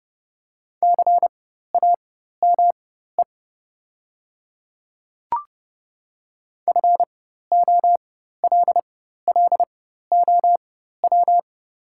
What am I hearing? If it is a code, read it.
Morse code, heard as a keyed tone, about 30 wpm, spaced as for 14 wpm, 700 Hz. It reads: CAME FOLLOW
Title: 4x Character Spacing - Top 200 Words - 30wpm